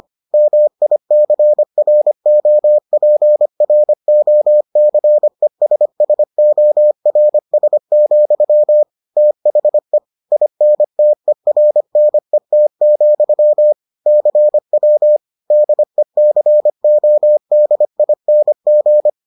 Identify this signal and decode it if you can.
Morse code, heard as a keyed tone, about 25 wpm, 600 Hz. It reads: MICROPROCESSORS, THE INTERNET, CW DECODING